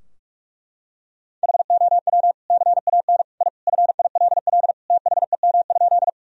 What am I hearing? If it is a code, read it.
Morse code, heard as a keyed tone, about 46 wpm, 700 Hz. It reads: HOW CAN I FILL THEM?
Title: Sentences from Top 1000 Words - Review Part 3 - 45wpm